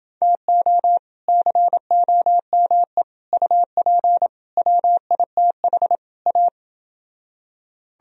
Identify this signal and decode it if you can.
Morse code, heard as a keyed tone, about 29 wpm, 700 Hz. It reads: TO COME UP WITH A